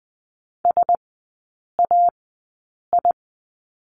Morse code, heard as a keyed tone, about 20 wpm, spaced as for 8 wpm, 700 Hz. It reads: SAI
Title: 041 Lesson 8e - Number 1 - Three Characters 20wpm